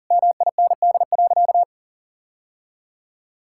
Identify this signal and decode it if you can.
Morse code, heard as a keyed tone, about 37 wpm, 700 Hz. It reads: MIND.
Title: Sentences from Top 1000 Words - Review Part 5 - 40wpm